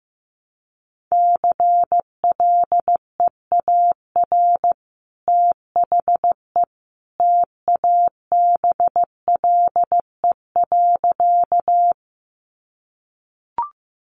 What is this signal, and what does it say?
Morse code, heard as a keyed tone, about 15 wpm, 700 Hz. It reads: CLEAR THE TABLE.
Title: Sentences from Top 500 Words 15wpm